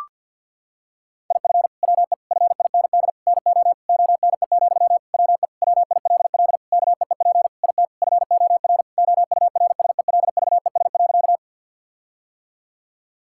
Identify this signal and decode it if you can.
Morse code, heard as a keyed tone, about 50 wpm, 700 Hz. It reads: IF WE FIND NO ONE, WE WILL KEEP IT FOR OURSELVES.
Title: Sentences from Top 1600 Words 50wpm